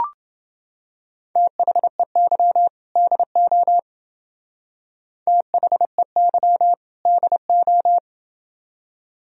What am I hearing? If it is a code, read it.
Morse code, heard as a keyed tone, about 31 wpm, 700 Hz. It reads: THEY DO THEY DO